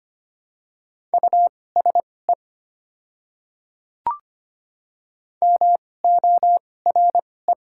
Morse code, heard as a keyed tone, about 25 wpm, spaced as for 18 wpm, 700 Hz. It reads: USE MORE
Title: 2x Character Spacing - Top 500 Words - 25wpm